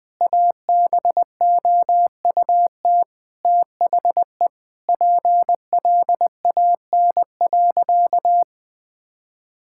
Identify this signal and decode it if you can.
Morse code, heard as a keyed tone, about 21 wpm, 700 Hz. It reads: ABOUT THE PLAN.
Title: Sentences with 4-Word Phrases - 20wpm